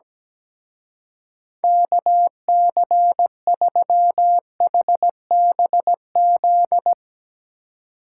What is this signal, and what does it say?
Morse code, heard as a keyed tone, about 17 wpm, 700 Hz. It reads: KC3HBZ
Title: Speed Racing - USA Call Signs 17wpm